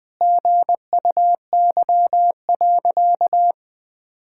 Morse code, heard as a keyed tone, about 20 wpm, 700 Hz. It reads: GUY.
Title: Sentences with 8-Word Phrases - 20wpm